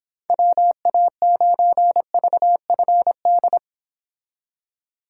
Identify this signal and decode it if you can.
Morse code, heard as a keyed tone, about 30 wpm, 700 Hz. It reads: WA9VFD